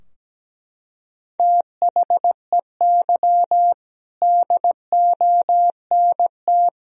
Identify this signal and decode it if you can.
Morse code, heard as a keyed tone, about 17 wpm, 700 Hz. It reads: THEY DONT
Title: Sentences from Top 1000 Words - Review Part 2 - 17wpm